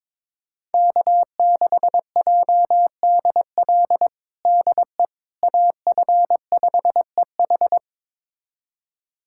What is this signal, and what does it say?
Morse code, heard as a keyed tone, about 22 wpm, 700 Hz. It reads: K6JDL DE AF5EH
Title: USA Callsign DE Callsign - 5 Characters - 22wpm